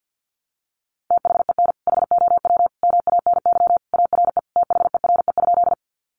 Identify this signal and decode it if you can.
Morse code, heard as a keyed tone, about 50 wpm, 700 Hz. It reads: THEN HOW MANY ARE THERE?
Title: Sentences from Top 100 Words 50wpm